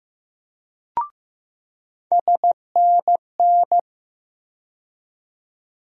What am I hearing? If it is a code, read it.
Morse code, heard as a keyed tone, about 15 wpm, 700 Hz. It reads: SNN
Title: USA Call Sign Postfixes - 3 Letters 15wpm